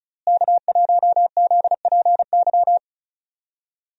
Morse code, heard as a keyed tone, about 36 wpm, 700 Hz. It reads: K1ZPY
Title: USA Call Signs - 5 Characters - Repeated 3x - 35wpm